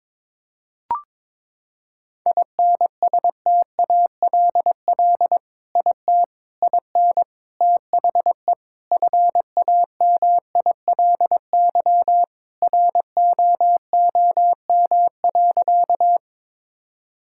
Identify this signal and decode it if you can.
Morse code, heard as a keyed tone, about 22 wpm, 700 Hz. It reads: INSTALL IT IN THE FAMILY ROOM.
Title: Sentences from Top 1600 Words 22wpm